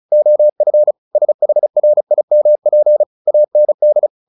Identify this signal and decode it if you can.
Morse code, heard as a keyed tone, about 35 wpm, 600 Hz. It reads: OF SHRIMP AND
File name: the old man and the sea / the old man and the sea 1 35wpm